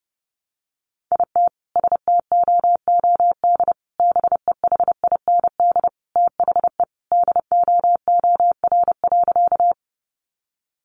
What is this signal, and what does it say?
Morse code, heard as a keyed tone, about 30 wpm, 700 Hz. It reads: IT STOOD BEHIND THE DOOR.
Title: Sentences from Top 500 Words 30wpm